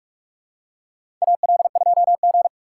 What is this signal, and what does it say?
Morse code, heard as a keyed tone, about 42 wpm, 700 Hz. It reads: AL2G
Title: USA Call Signs - 4 Characters - 4x Character Spacing - 45wpm